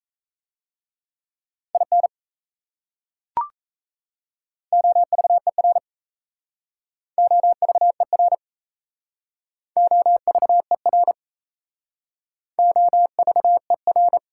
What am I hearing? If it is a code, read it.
Morse code, heard as a keyed tone, about 38 wpm, 700 Hz. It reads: IN OVER OVER OVER OVER